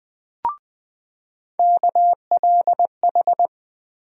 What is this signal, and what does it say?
Morse code, heard as a keyed tone, about 20 wpm, 700 Hz. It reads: KLH
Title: USA Call Sign Postfixes - 3 Letters 20wpm